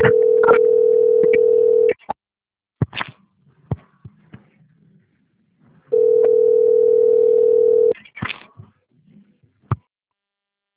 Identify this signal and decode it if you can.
Ringback tone (North America), 440 + 480 Hz.
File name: rg-1600-unknown- - - .53815